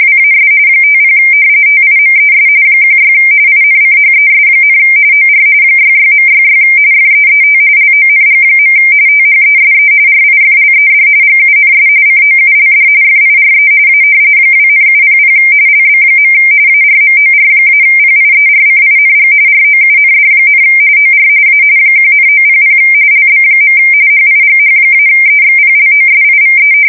RTTY, about 45.45 baud. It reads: WAS 50 TIMES 16 INCHES, OR 66.66 FEET LONG. THESE RIFLED GUNS COULD FIRE A 2,700 LB. ARMOR-PIERCING ROUND, OR A 1,900 LB. HIGH-EXPLOSIVE
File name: fsk170 / battleshipnj